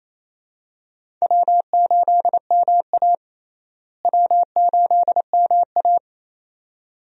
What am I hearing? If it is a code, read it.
Morse code, heard as a keyed tone, about 28 wpm, 700 Hz. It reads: W8MA W8MA
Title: USA Call Signs - 4 Characters - Repeated 2x - 28wpm